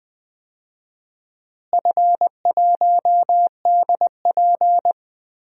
Morse code, heard as a keyed tone, about 20 wpm, 700 Hz. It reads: F1DP
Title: DXCC-100 Call Signs - 4 Characters - Repeated 2x - 20wpm